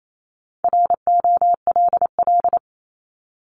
Morse code, heard as a keyed tone, about 28 wpm, 700 Hz. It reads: ROLL
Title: Top 1751-2000 Words 28wpm